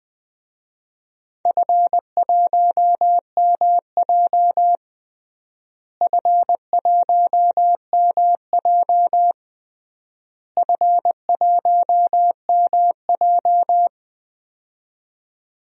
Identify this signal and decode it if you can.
Morse code, heard as a keyed tone, about 21 wpm, 700 Hz. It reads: F1MJ F1MJ F1MJ
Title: DXCC-100 Call Signs - 4 Characters - Repeated 3x - 20wpm